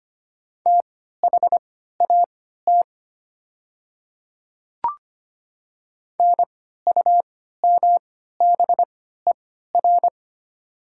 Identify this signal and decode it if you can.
Morse code, heard as a keyed tone, about 25 wpm, spaced as for 14 wpm, 700 Hz. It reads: THAT NUMBER
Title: 3x Character Spacing - Top 200 Words - 25wpm